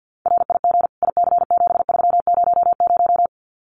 Morse code, heard as a keyed tone, about 50 wpm, 700 Hz. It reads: RIG IC7300
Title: Common QSO Phrases Encore 50wpm